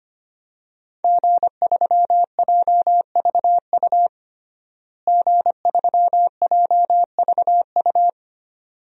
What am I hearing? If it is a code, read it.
Morse code, heard as a keyed tone, about 25 wpm, 700 Hz. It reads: G3JVU G3JVU
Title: DXCC-100 Call Signs - 5 Characters - Repeated 2x - 25wpm